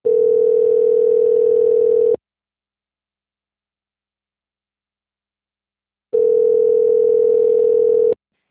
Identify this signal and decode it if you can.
Ringback tone (North America), 440 + 480 Hz.